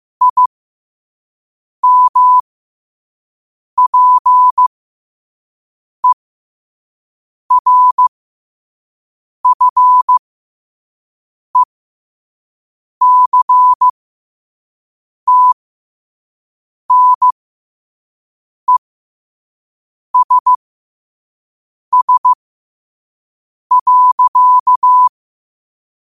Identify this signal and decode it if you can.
Morse code, heard as a keyed tone, about 15 wpm, spaced as for 5 wpm, 1000 Hz. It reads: IMPERFECTNESS.
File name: qotdfiles / 5wpm